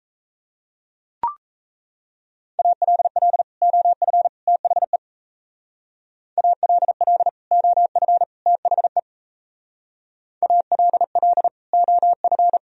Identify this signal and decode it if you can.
Morse code, heard as a keyed tone, about 34 wpm, 700 Hz. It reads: ALL OF THE ALL OF THE ALL OF
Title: Speed-Racing - Sets of 3 Words 28wpm